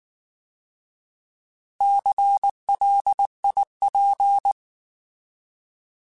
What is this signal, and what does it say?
Morse code, heard as a keyed tone, about 19 wpm, 800 Hz. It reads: CLIP